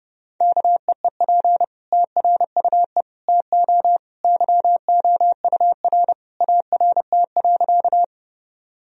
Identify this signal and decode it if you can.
Morse code, heard as a keyed tone, about 30 wpm, 700 Hz. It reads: KEEP TRUE TO YOUR ART.
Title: Sentences from Top 1000 Words - Review Part 3 - 30wpm